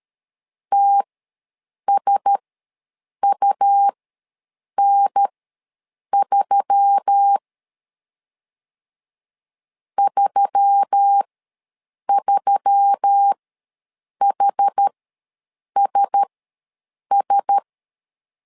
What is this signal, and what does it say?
Morse code, heard as a keyed tone, about 13 wpm, spaced as for 7 wpm, 800 Hz. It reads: TSUN3 33HSS
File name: LEZIONI / 4 Q1S3 ws 12wpm 2038